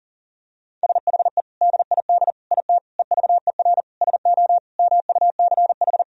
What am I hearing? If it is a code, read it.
Morse code, heard as a keyed tone, about 40 wpm, 700 Hz. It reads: SHE DID IT EVER SO MUCH
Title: Sentences from Top 400 Words and Abbreviations 40wpm